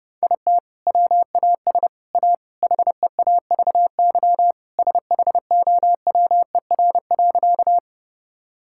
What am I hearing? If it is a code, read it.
Morse code, heard as a keyed tone, about 31 wpm, 700 Hz. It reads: IT WAS A HEAVY SHOWER.
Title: Sentences from Top 1600 Words 30wpm